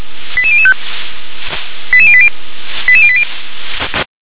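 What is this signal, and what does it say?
Five-tone selective call (ZVEI), heard as quick runs of five tones, tones of about 70 ms. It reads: 88005 88080 88080
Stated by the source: ZVEI five-tone selective call received on an SDR.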